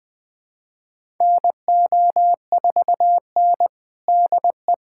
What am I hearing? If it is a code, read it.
Morse code, heard as a keyed tone, about 21 wpm, 700 Hz. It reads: NO4N DE
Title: USA Callsign DE Callsign - 4 Characters - 20wpm